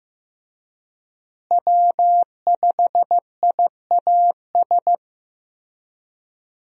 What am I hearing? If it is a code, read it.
Morse code, heard as a keyed tone, about 15 wpm, 700 Hz. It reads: W5IAS
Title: USA Call Signs - 5 Characters - Repeated 3x - 15wpm